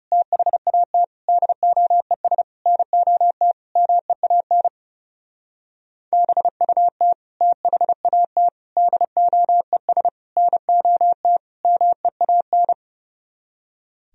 Morse code, heard as a keyed tone, about 35 wpm, 700 Hz. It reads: THAT DOES NOT MEAN BUT THAT DOES NOT MEAN
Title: Speed-Racing - Sets of 5 Words 30wpm